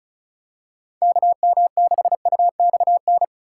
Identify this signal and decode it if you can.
Morse code, heard as a keyed tone, about 36 wpm, 700 Hz. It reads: KM6UXN
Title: USA Call Signs - 6 Characters - 4x Character Spacing - 35wpm